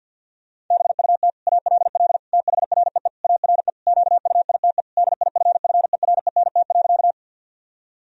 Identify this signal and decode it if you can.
Morse code, heard as a keyed tone, about 52 wpm, 700 Hz. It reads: BUT ALL THREE ARE QUITE DIFFERENT.
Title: Sentences from Top 1000 Words - Review Part 3 - 50wpm